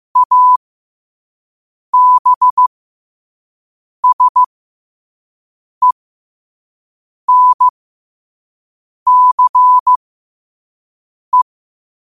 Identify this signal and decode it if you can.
Morse code, heard as a keyed tone, about 15 wpm, spaced as for 5 wpm, 1000 Hz. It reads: ABSENCE
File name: qotdfiles / 5wpm